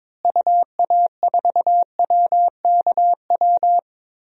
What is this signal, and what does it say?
Morse code, heard as a keyed tone, about 21 wpm, 700 Hz. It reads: UA4WKW
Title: DXCC-100 Call Signs - 6 Characters - Repeated 3x - 22wpm